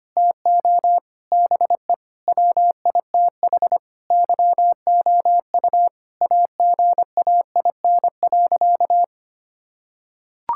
Morse code, heard as a keyed tone, about 25 wpm, 700 Hz. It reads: TO BE WITH YOU AGAIN.
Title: Sentences from Top 1600 Words 25wpm